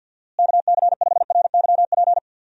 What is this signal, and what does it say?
Morse code, heard as a keyed tone, about 50 wpm, 700 Hz. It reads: KC5RYP
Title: USA Call Signs - 6 Characters - Repeated 2x - 50wpm